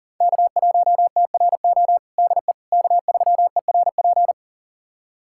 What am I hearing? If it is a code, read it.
Morse code, heard as a keyed tone, about 40 wpm, 700 Hz. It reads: K1TRO DE K3ERP
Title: USA Callsign DE Callsign 40wpm